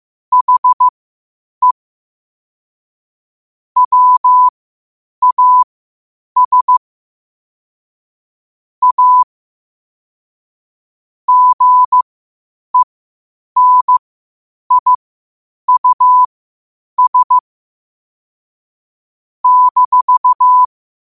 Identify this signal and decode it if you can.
Morse code, heard as a keyed tone, about 15 wpm, spaced as for 8 wpm, 1000 Hz. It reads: HE WAS A GENIUS -
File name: qotdfiles / 8wpm